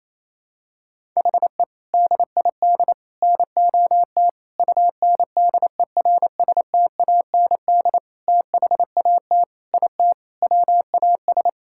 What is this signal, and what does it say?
Morse code, heard as a keyed tone, about 28 wpm, 700 Hz. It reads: HE DID NOT UNDERSTAND THAT IT WAS
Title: Sentences from Top 1000 Words - Review Part 5 - 28wpm